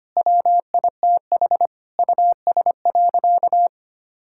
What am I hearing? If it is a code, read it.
Morse code, heard as a keyed tone, about 25 wpm, 700 Hz. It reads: WITH US.
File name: practice / 25wpm / Sentences with 4-Word Phrases - 25wpm